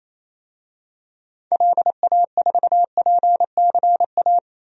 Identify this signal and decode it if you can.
Morse code, heard as a keyed tone, about 28 wpm, 700 Hz. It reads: LA4PCA
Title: DXCC-100 Call Signs - 6 Characters - 28wpm